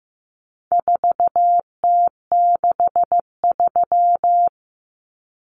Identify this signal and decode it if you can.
Morse code, heard as a keyed tone, about 15 wpm, 700 Hz. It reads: 4T63
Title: Common Cut Numbers - 4-Digit 15wpm